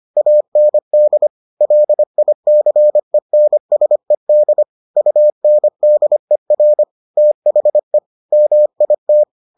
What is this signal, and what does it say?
Morse code, heard as a keyed tone, about 25 wpm, 600 Hz. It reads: AND LICENSED UNDER THE MIT